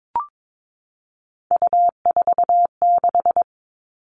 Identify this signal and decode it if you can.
Morse code, heard as a keyed tone, about 22 wpm, 700 Hz. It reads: U46
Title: All Cut Numbers - 3-Digit 22wpm